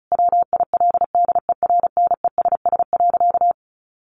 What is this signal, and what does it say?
Morse code, heard as a keyed tone, about 34 wpm, 700 Hz. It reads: WILDERNESS.
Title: A Sound Of Thunder 35wpm